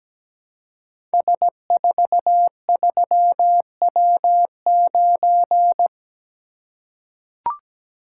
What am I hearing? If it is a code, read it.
Morse code, heard as a keyed tone, about 17 wpm, 700 Hz. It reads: S43W9
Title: Code Groups - 5 Letter:Numbers - 17wpm